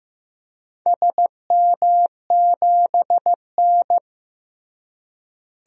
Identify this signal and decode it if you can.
Morse code, heard as a keyed tone, about 15 wpm, 700 Hz. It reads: SM7N
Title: DXCC-100 Call Signs - 4 Characters - Repeated 2x - 15wpm